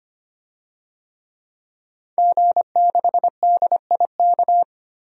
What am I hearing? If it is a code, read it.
Morse code, heard as a keyed tone, about 25 wpm, 700 Hz. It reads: G6DIK
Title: DXCC-100 Call Signs - 5 Characters - Repeated 3x - 25wpm